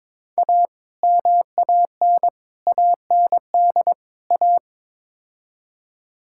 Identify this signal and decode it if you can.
Morse code, heard as a keyed tone, about 22 wpm, 700 Hz. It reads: A MAN AND A
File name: practice / 22wpm / Build A Phrase - 5-Word Phrases - 22wpm